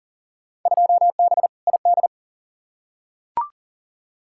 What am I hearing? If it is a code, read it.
Morse code, heard as a keyed tone, about 40 wpm, 700 Hz. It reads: 2B ID
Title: Field Day Exchanges Encore - 40wpm